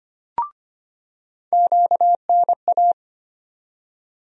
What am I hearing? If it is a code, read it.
Morse code, heard as a keyed tone, about 25 wpm, 700 Hz. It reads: QNA
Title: USA Call Sign Postfixes - 3 Letters 25wpm